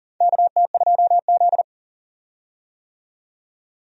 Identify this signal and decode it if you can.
Morse code, heard as a keyed tone, about 40 wpm, 700 Hz. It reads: KT2Z
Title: USA Call Signs - 4 Characters - 40wpm